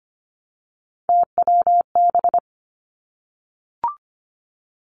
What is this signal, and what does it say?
Morse code, heard as a keyed tone, about 25 wpm, 700 Hz. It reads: TWB
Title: Three Challenging Letter-Number Combinations 25wpm